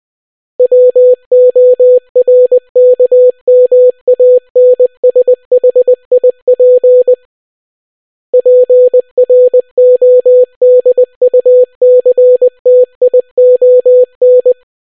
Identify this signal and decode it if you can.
Morse code, heard as a keyed tone, about 20 wpm, 500 Hz. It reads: WORKMANSHIP PRODUCTION